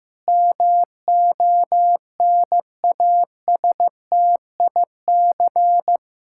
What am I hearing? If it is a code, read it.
Morse code, heard as a keyed tone, about 14 wpm, 700 Hz. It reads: MONASTIC
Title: Pangram Challenge v2 - 15wpm